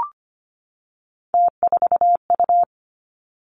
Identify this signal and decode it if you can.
Morse code, heard as a keyed tone, about 25 wpm, 700 Hz. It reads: T4U
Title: Three Challenging Letter-Number Combinations 25wpm